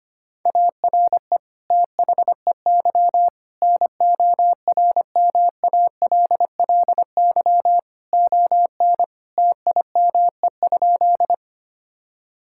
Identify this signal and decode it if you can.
Morse code, heard as a keyed tone, about 25 wpm, 700 Hz. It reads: ARE THEY NORMALLY ON TIME?
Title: Sentences from Top 1000 Words - Review Part 6 - 25wpm